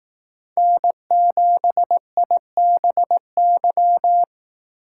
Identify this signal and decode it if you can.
Morse code, heard as a keyed tone, about 18 wpm, 700 Hz. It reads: N7IBY
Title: Speed Racing - USA 1x3 Call Signs 15wpm